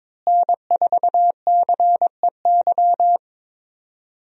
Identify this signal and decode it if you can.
Morse code, heard as a keyed tone, about 22 wpm, 700 Hz. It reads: N4CEY
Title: USA Call Signs - 5 Characters - Repeated 2x - 22wpm